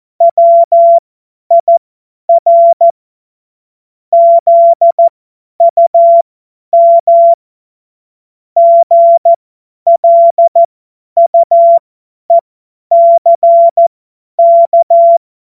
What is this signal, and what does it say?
Morse code, heard as a keyed tone, about 14 wpm, spaced as for 10 wpm, 680 Hz. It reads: WIR ZUM GLUECK
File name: Morsekurs / Witze / Witz 17 Juni 2025 50 F70